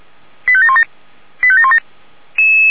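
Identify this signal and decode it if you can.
Five-tone selective call (ZVEI), heard as quick runs of five tones, tones of about 70 ms. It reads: 85717 85717
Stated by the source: ZVEI five-tone selective call received on an SDR.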